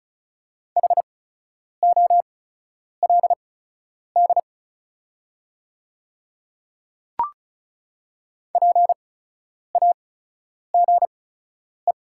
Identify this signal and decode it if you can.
Morse code, heard as a keyed tone, about 36 wpm, spaced as for 10 wpm, 700 Hz. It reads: HOLD PAGE
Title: 8x Character Spacing - Top 500 Words - 35wpm